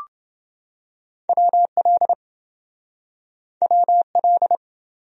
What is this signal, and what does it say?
Morse code, heard as a keyed tone, about 26 wpm, 700 Hz. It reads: WL WL